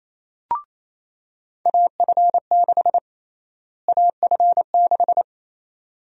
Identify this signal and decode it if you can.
Morse code, heard as a keyed tone, about 28 wpm, 700 Hz. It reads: AF6 AF6
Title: USA Call Sign Prefixes - Repeated 2x - 28wpm